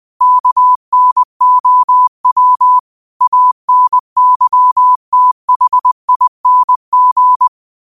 Morse code, heard as a keyed tone, about 20 wpm, 1000 Hz. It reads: KNOW ANYTHING